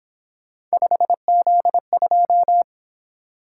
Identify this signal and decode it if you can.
Morse code, heard as a keyed tone, about 27 wpm, 700 Hz. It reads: 5Z2
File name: Speed-Racing - Three Letter-Number Combinations - 22wpm